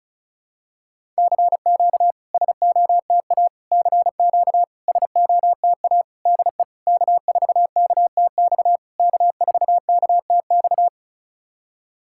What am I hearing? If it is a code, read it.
Morse code, heard as a keyed tone, about 34 wpm, 690 Hz. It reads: CQ SOTA CQ SOTA DE K4KTX K4KTX
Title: SOTA QSO Practice 35wpm